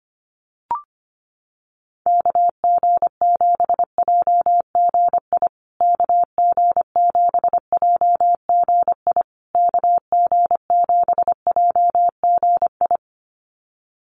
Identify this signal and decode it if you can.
Morse code, heard as a keyed tone, about 25 wpm, 700 Hz. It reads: KG7JGI KG7JGI KG7JGI
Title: USA Call Signs Repeated 3x 25wpm